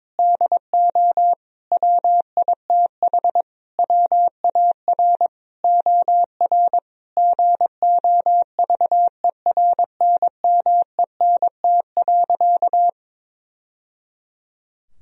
Morse code, heard as a keyed tone, about 22 wpm, 700 Hz. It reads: DO WITH WAR OR GOVERNMENT.
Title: Sentences from Top 1000 Words - Review Part 3 - 22wpm